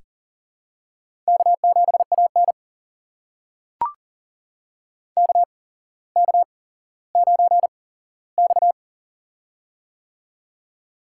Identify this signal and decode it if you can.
Morse code, heard as a keyed tone, about 37 wpm, spaced as for 17 wpm, 700 Hz. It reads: K7AN KK9X
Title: USA Call Signs - 4 Characters - 8x Character Spacing - 40wpm